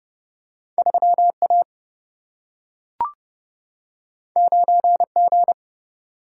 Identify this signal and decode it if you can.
Morse code, heard as a keyed tone, about 30 wpm, 700 Hz. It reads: 3 A 9 G
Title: Single-Letter Single-Number 30wpm